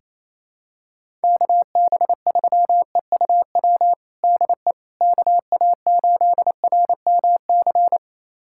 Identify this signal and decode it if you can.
Morse code, heard as a keyed tone, about 28 wpm, 700 Hz. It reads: KB3EUW DE KA8RMC
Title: USA Callsign DE Callsign - 6 Characters - 28wpm